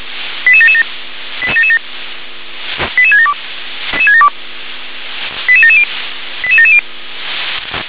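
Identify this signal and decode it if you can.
Five-tone selective call (ZVEI), heard as quick runs of five tones, tones of about 70 ms. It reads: 88777 88777 88672 88770 88770
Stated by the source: ZVEI five-tone selective call received via SDR.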